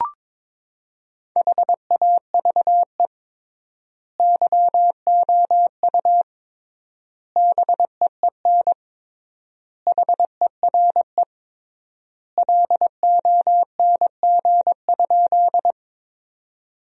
Morse code, heard as a keyed tone, about 22 wpm, 700 Hz. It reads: HAVE YOU BEEN HERE LONG?
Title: 3x Word-Spacing - Sentences from Top 200 Words - 22wpm